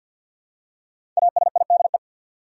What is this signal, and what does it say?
Morse code, heard as a keyed tone, about 53 wpm, 700 Hz. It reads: ASIDE
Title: Five Letter Words - 50wpm